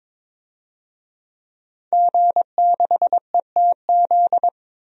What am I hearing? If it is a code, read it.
Morse code, heard as a keyed tone, about 22 wpm, 700 Hz. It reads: G6ETZ
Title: DXCC-100 Call Signs - 5 Characters - 22wpm